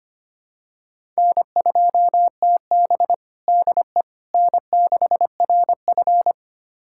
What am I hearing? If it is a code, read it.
Morse code, heard as a keyed tone, about 25 wpm, 700 Hz. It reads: N2TB DE N6RF
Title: USA Callsign DE Callsign - 4 Characters - 25wpm